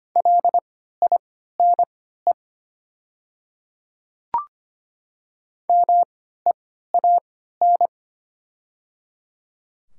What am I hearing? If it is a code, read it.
Morse code, heard as a keyed tone, about 25 wpm, spaced as for 14 wpm, 700 Hz. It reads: LINE MEAN
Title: 3x Character Spacing - Top 200 Words - 25wpm